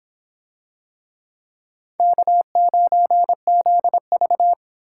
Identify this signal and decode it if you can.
Morse code, heard as a keyed tone, about 24 wpm, 700 Hz. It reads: K9ZV